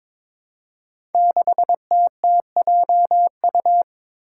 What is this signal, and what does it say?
Morse code, heard as a keyed tone, about 23 wpm, 700 Hz. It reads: 6TTJU
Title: Five Challenging Letter-Number Combinations 22wpm